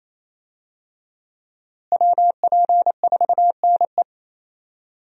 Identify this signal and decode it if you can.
Morse code, heard as a keyed tone, about 28 wpm, 700 Hz. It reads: WP4NE
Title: USA Call Signs - 5 Characters - Repeated 2x - 28wpm